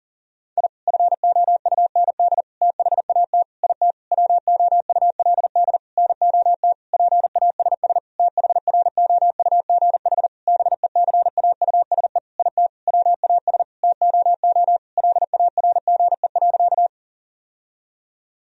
Morse code, heard as a keyed tone, about 40 wpm, 700 Hz. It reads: I FOUND THAT IT WOULD NOT PASS THROUGH BECAUSE IT WAS TOO LARGE.
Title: Sentences from Top 1000 Words - Review Part 1 - 40wpm